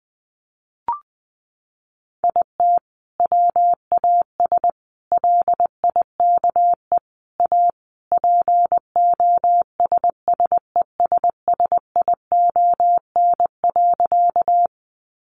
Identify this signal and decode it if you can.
Morse code, heard as a keyed tone, about 20 wpm, 700 Hz. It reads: IT WAS LIKE A POSSESSION.
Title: Sentences from Top 1400 Words 20wpm